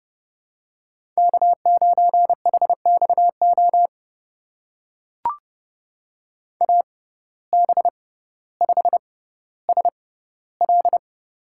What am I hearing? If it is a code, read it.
Morse code, heard as a keyed tone, about 29 wpm, spaced as for 15 wpm, 700 Hz. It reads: K9HXO AB5SL
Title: USA Call Signs - 5 Characters - 6x Character Spacing - 30wpm